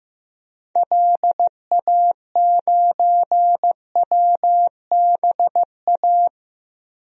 Morse code, heard as a keyed tone, about 14 wpm, 700 Hz. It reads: LA9WBA
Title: DXCC-100 Call Signs - 6 Characters - Repeated 3x - 15wpm